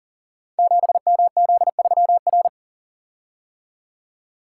Morse code, heard as a keyed tone, about 38 wpm, 700 Hz. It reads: 7MZ3R
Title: Code Groups - 5 Letter:Numbers - 40wpm